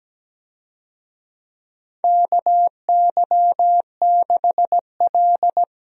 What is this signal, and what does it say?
Morse code, heard as a keyed tone, about 17 wpm, 700 Hz. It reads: KY6L
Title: USA Call Signs - 4 Characters - Repeated 2x - 17wpm